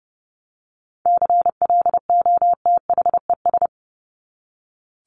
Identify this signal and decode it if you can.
Morse code, heard as a keyed tone, about 30 wpm, 700 Hz. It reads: CLOTHES
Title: Top 1751-2000 Words 30wpm